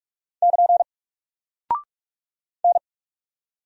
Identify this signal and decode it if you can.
Morse code, heard as a keyed tone, about 44 wpm, 700 Hz. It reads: <KN> N
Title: Single Character (Letter:Number:Punctuation:Prosign) Alt 45wpm